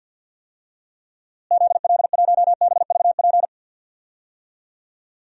Morse code, heard as a keyed tone, about 52 wpm, 700 Hz. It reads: ZL1BVP
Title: DXCC-100 Call Signs - 6 Characters - Repeated 2x - 50wpm